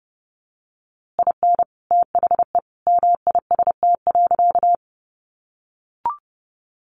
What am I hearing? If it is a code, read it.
Morse code, heard as a keyed tone, about 30 wpm, 700 Hz. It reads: IN THE MIST.
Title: A Sound Of Thunder 30wpm